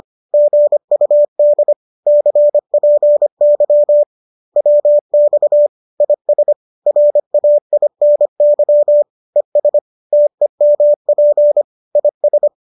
Morse code, heard as a keyed tone, about 25 wpm, 590 Hz. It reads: GUD CPY WX IS RAINY ES TEMP IS